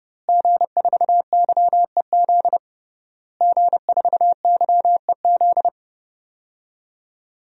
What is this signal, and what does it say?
Morse code, heard as a keyed tone, about 30 wpm, 700 Hz. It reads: G4YEZ G4YEZ
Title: DXCC-100 Call Signs - 5 Characters - Repeated 3x - 30wpm